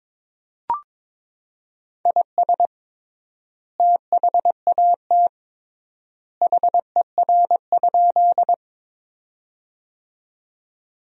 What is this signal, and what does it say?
Morse code, heard as a keyed tone, about 22 wpm, 700 Hz. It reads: IS THAT HER?
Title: 3x Word-Spacing - Sentences from Top 100 Words - 22wpm